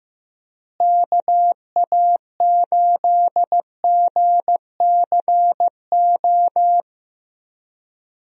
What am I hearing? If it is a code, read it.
Morse code, heard as a keyed tone, about 16 wpm, 700 Hz. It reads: KA8GCO
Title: USA Call Signs - 6 Characters - Repeated 2x - 15wpm